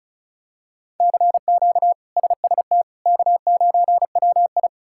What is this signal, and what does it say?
Morse code, heard as a keyed tone, about 36 wpm, 700 Hz. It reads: CQ SST K9WI
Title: K1USN SST QSO Practice - 35wpm